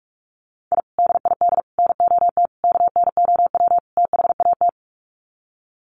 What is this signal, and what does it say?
Morse code, heard as a keyed tone, about 44 wpm, 700 Hz. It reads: I DID NOT KNOW THAT
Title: Sets of 5 Words 45wpm